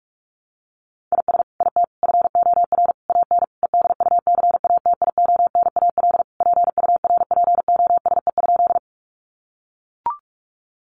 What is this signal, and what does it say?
Morse code, heard as a keyed tone, about 44 wpm, 700 Hz. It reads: IS IT FOR AN EDUCATIONAL PURPOSE?
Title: Sentences from Top 1200 Words 45wpm